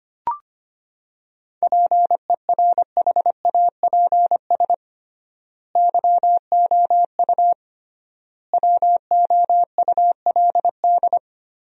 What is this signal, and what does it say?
Morse code, heard as a keyed tone, about 25 wpm, 700 Hz. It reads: PERHAPS YOU WOULD
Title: 3x Word-Spacing - Sentences from Top 500 Words - 25wpm